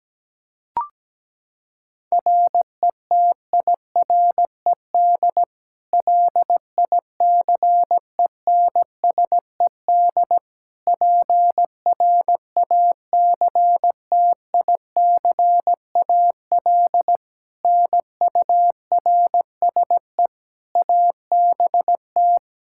Morse code, heard as a keyed tone, about 17 wpm, 700 Hz. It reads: RETIRED LICENSED PRACTICAL NURSE ABT
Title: K1USN SST QSO Practice 17wpm